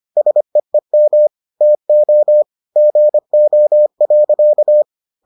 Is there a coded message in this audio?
Yes: Morse code from 0.2 to 4.8 s, heard as a keyed tone, about 25 wpm, 600 Hz. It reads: SEEM TO GO.